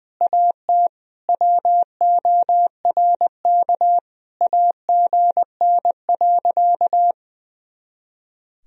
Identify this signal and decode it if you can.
Morse code, heard as a keyed tone, about 20 wpm, 700 Hz. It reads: AT WORK AGN.
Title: Sentences from Top 200 Words and Abbreviations 20wpm